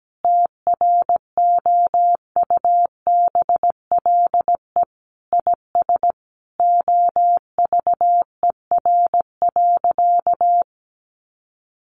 Morse code, heard as a keyed tone, about 17 wpm, 700 Hz. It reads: TROUBLE IS OVER.
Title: Sentences from Top 1200 Words 17wpm